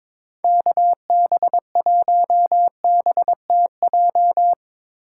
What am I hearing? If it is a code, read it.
Morse code, heard as a keyed tone, about 22 wpm, 700 Hz. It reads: KB1BTJ
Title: USA Call Signs - 6 Characters - 8x Character Spacing - 22wpm